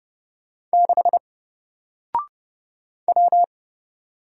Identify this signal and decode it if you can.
Morse code, heard as a keyed tone, about 30 wpm, 700 Hz. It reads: 6 W
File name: Single-Letter-Number-30wpm